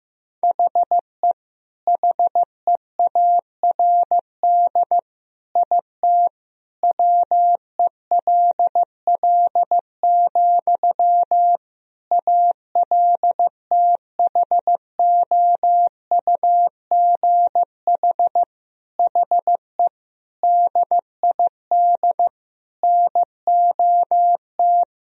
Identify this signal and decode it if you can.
Morse code, heard as a keyed tone, about 15 wpm, 700 Hz. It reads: HE HEARD IT WELL, ALTHOUGH HE DID NOT
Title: Sentences from Top 1000 Words - Review Part 3 - 15wpm